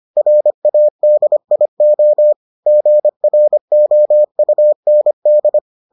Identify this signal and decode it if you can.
Morse code, heard as a keyed tone, about 25 wpm, 600 Hz. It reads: RADIO GROUND